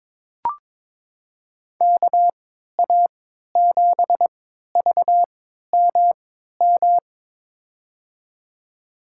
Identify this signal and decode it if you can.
Morse code, heard as a keyed tone, about 22 wpm, spaced as for 13 wpm, 700 Hz. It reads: KA7VMM
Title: USA Call Signs - 6 Characters - 3x Character Spacing - 22wpm